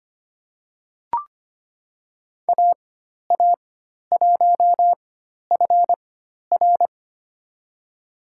Morse code, heard as a keyed tone, about 25 wpm, spaced as for 12 wpm, 700 Hz. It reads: AA1FR
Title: USA Call Signs - 4x Character Spacing - 25wpm